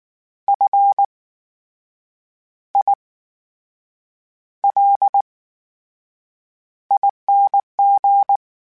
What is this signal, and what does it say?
Morse code, heard as a keyed tone, about 19 wpm, spaced as for 7 wpm, 800 Hz. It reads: FILING
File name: Special INGs / Special INGs-mp3s / VerySlowToFaster Ing @18